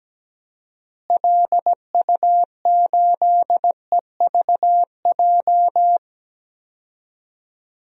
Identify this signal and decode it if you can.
Morse code, heard as a keyed tone, about 17 wpm, 700 Hz. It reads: LU8EVJ
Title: DXCC-100 Call Signs - 6 Characters - Repeated 2x - 17wpm